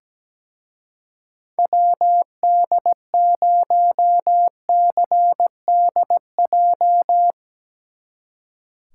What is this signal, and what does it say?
Morse code, heard as a keyed tone, about 19 wpm, 700 Hz. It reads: WD0CDJ
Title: Speed Racing - USA Call Signs 17wpm